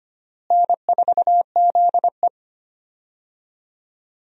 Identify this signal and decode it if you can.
Morse code, heard as a keyed tone, about 25 wpm, 700 Hz. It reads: N4ZE
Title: USA Call Signs - 4 Characters - Repeated 3x - 25wpm